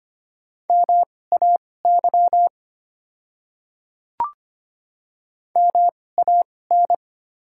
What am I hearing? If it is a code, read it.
Morse code, heard as a keyed tone, about 25 wpm, spaced as for 18 wpm, 700 Hz. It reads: MAY MAN
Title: 2x Character Spacing - Top 200 Words - 25wpm